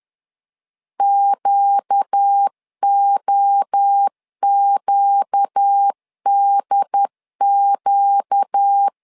Morse code, heard as a keyed tone, about 10 wpm, 800 Hz. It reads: QOQDQ
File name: LEZIONI / 4 Q 10wpm 2210